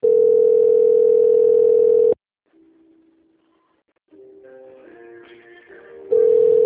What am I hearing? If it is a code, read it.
Ringback tone (North America), 440 + 480 Hz.